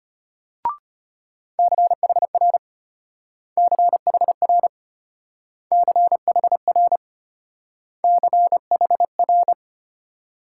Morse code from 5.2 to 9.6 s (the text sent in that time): CHR CHR